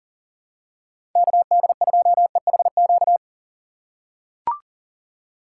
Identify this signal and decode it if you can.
Morse code, heard as a keyed tone, about 40 wpm, 700 Hz. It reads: KD2EHQ
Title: USA Call Signs - 2x Character Spacing - 40wpm